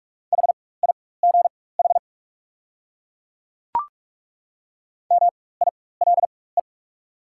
Morse code, heard as a keyed tone, about 43 wpm, spaced as for 21 wpm, 700 Hz. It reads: HIGH MILE
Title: 4x Character Spacing - Top 500 Words - 45wpm